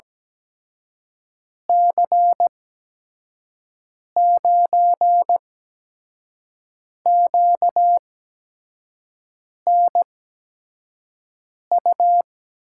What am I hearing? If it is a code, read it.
Morse code, heard as a keyed tone, about 17 wpm, spaced as for 5 wpm, 700 Hz. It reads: C9QNU
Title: USA Call Signs - 6 Characters - 8x Character Spacing - 17wpm